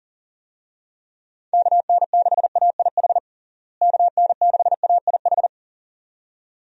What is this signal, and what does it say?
Morse code, heard as a keyed tone, about 40 wpm, 700 Hz. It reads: KN6AIH KN6AIH
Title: USA Call Signs - 6 Characters - Repeated 2x - 40wpm